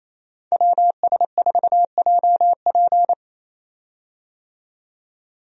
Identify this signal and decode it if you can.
Morse code, heard as a keyed tone, about 28 wpm, 700 Hz. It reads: WS4JP
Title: USA Call Signs - 5 Characters - 28wpm